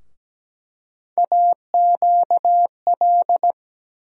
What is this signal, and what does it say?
Morse code, heard as a keyed tone, about 17 wpm, 700 Hz. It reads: AQL
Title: USA Call Sign Postfixes - 3 Letters 17wpm